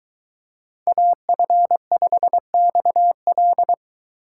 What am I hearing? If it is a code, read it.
Morse code, heard as a keyed tone, about 23 wpm, 700 Hz. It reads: AF5XL